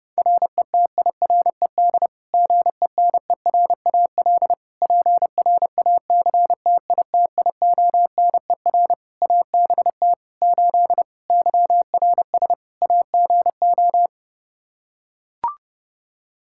Morse code, heard as a keyed tone, about 30 wpm, 700 Hz. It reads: RETIRED GENERAL PRACTITIONER ABT 8 YRS AGO
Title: Practice QSO - Exchange 3 - More Info - 30wpm